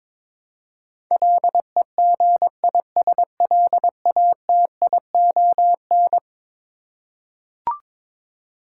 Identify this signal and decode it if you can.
Morse code, heard as a keyed tone, about 22 wpm, 700 Hz. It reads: LEGISLATION
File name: practice / 22wpm / Eleven Letter Words - 22wpm